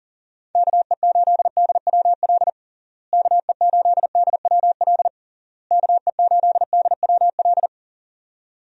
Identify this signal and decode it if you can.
Morse code, heard as a keyed tone, about 40 wpm, 700 Hz. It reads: KE8DWL KE8DWL KE8DWL
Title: USA Call Signs - 6 Characters - Repeated 3x - 40wpm